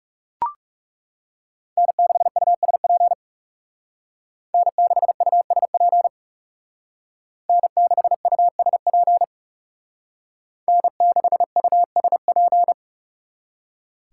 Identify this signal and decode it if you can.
Morse code, heard as a keyed tone, about 37 wpm, 700 Hz. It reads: N6USP N6USP N6USP N6USP
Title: Speed Racing - USA 1x3 Call Signs 30wpm